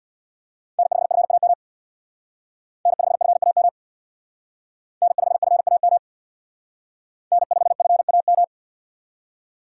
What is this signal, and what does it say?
Morse code, heard as a keyed tone, about 64 wpm, 700 Hz. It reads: N5FAM N5FAM N5FAM N5FAM